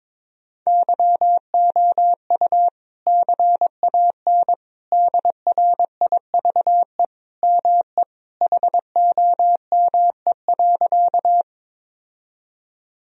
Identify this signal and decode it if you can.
Morse code, heard as a keyed tone, about 22 wpm, 700 Hz. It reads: YOU CAN DRIVE ME HOME.
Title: Sentences from Top 1000 Words - Review Part 2 - 22wpm